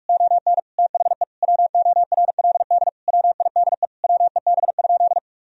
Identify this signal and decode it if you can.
Morse code, heard as a keyed tone, about 41 wpm, 700 Hz. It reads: ON THE WORLD WIDE WEB?
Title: Sentences from Top 1000 Words - Review Part 4 - 45wpm